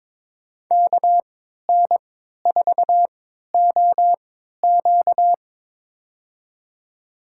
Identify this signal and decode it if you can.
Morse code, heard as a keyed tone, about 22 wpm, spaced as for 13 wpm, 700 Hz. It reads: KN4OQ
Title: USA Call Signs - 5 Characters - 3x Character Spacing - 22wpm